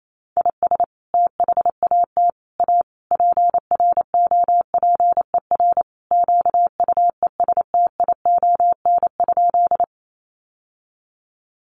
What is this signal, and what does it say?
Morse code, heard as a keyed tone, about 28 wpm, 700 Hz. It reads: IS THAT A PROPER QUESTION?
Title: Sentences from Top 1100 Words 28wpm